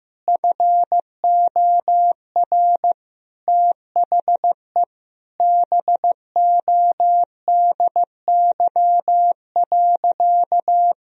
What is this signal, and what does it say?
Morse code, heard as a keyed tone, about 16 wpm, 700 Hz. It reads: FOR THE BODY.
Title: Sentences from Top 1000 Words - Review Part 4 - 15wpm